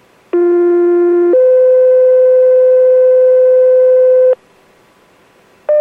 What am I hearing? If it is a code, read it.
Two-tone page, heard as a one-second tone, then a three-second tone, about 350 then 510 Hz. It is heard from 0.3 to 4.3 s.